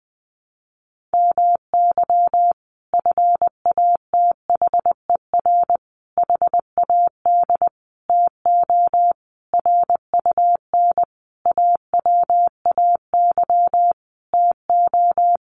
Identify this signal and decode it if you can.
Morse code, heard as a keyed tone, about 20 wpm, 700 Hz. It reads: MY FATHER HAD TO RUN AWAY TO
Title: Sentences from Top 1100 Words 20wpm